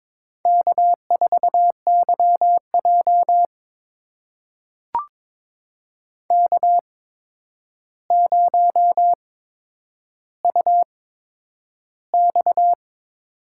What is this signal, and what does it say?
Morse code, heard as a keyed tone, about 22 wpm, spaced as for 9 wpm, 700 Hz. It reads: K4YJ K0UX
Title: USA Call Signs - 4 Characters - 8x Character Spacing - 22wpm